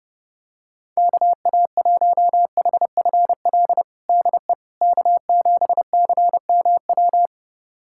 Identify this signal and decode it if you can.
Morse code, heard as a keyed tone, about 30 wpm, 700 Hz. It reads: KA1HFL DE K7CMW
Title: USA Callsign DE Callsign 30wpm